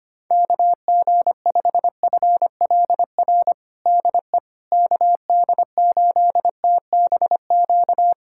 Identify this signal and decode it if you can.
Morse code, heard as a keyed tone, about 25 wpm, 700 Hz. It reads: KG5FLR DE KD8TBQ
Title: USA Callsign DE Callsign 25wpm